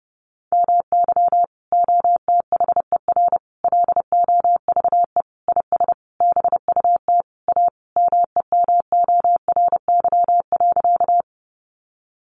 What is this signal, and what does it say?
Morse code, heard as a keyed tone, about 30 wpm, 700 Hz. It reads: MY OTHER LOVE IS BUT A MEMORY.
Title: Sentences from Top 1300 Words 30wpm